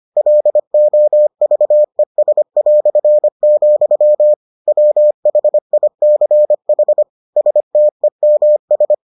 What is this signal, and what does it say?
Morse code, heard as a keyed tone, about 25 wpm, 600 Hz. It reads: LOVES", WHICH STEMS